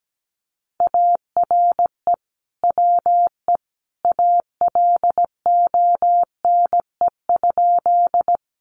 Morse code, heard as a keyed tone, about 17 wpm, 700 Hz. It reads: ARE WE ALONE?
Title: Sentences from Top 1000 Words 17wpm